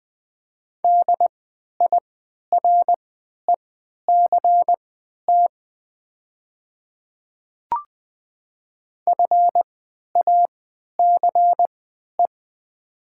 Morse code, heard as a keyed tone, about 20 wpm, spaced as for 11 wpm, 700 Hz. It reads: DIRECT FACE
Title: 3x Character Spacing - Top 500 Words - 20wpm